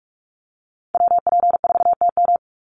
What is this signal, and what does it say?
Morse code, heard as a keyed tone, about 44 wpm, 700 Hz. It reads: WP4TM
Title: USA Call Signs 45wpm